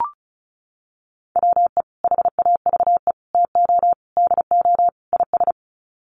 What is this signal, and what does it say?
Morse code, heard as a keyed tone, about 34 wpm, 700 Hz. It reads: WE HAVE TO DO IS
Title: Sets of 5 Words Encore 35wpm